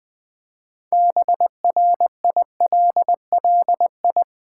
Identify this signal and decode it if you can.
Morse code, heard as a keyed tone, about 20 wpm, 700 Hz. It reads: BRILLI
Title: Build A Word - 3 Letters At A Time - 9-Letter Words - 20wpm